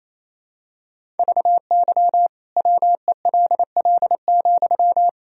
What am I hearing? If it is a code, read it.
Morse code, heard as a keyed tone, about 28 wpm, 700 Hz. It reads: VY WELL,
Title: Sentences from Top 200 Words and Abbreviations 28wpm